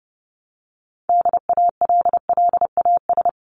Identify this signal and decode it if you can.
Morse code, heard as a keyed tone, about 30 wpm, 700 Hz. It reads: DALLAS
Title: CWT Exchange 30wpm